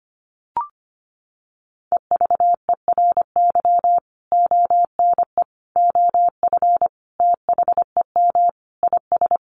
Morse code, heard as a keyed tone, about 25 wpm, 700 Hz. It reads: EVERY ONE OF THEM IS
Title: Sentences from Top 1100 Words 25wpm